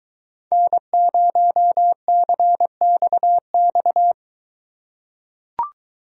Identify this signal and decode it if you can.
Morse code, heard as a keyed tone, about 21 wpm, 700 Hz. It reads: N0CXX